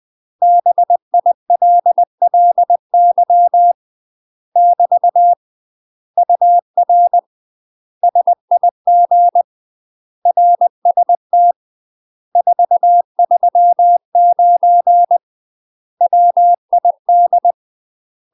Morse code, heard as a keyed tone, about 20 wpm, 700 Hz. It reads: BILLY <BT> UR SIG RST 439 WID